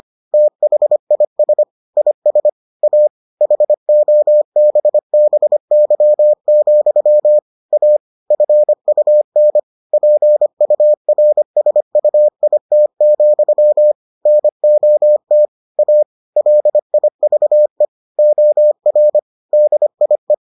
Morse code, heard as a keyed tone, about 25 wpm, 600 Hz. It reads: THIS IS A HOBBY, A FUN PURSUIT, NOT A LIVE OR DIE